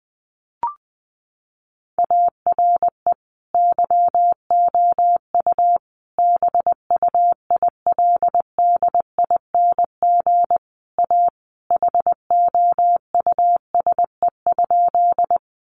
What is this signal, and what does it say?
Morse code, heard as a keyed tone, about 20 wpm, 700 Hz. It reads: ARE YOU BUILDING A HOUSE?
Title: Sentences from Top 700 Words 20wpm